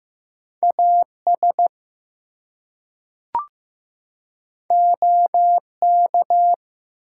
Morse code, heard as a keyed tone, about 15 wpm, 700 Hz. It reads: AS OK
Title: Two Letter Words - 15wpm